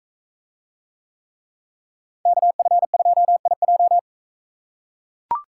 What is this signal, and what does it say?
Morse code, heard as a keyed tone, about 43 wpm, 700 Hz. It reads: KF2IJ